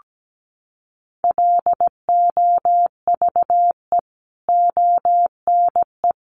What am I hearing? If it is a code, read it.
Morse code, heard as a keyed tone, about 17 wpm, 700 Hz. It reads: LOVE ONE
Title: Sentences from Top 600 Words 17wpm